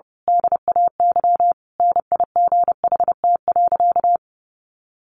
Morse code, heard as a keyed tone, about 30 wpm, 700 Hz. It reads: DAY NIGHT.
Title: A Sound Of Thunder 30wpm